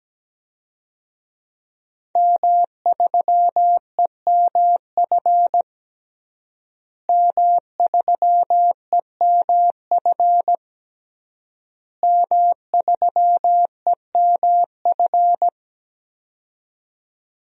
Morse code, heard as a keyed tone, about 17 wpm, 700 Hz. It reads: M3EMF M3EMF M3EMF
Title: DXCC-100 Call Signs - 5 Characters - Repeated 3x - 17wpm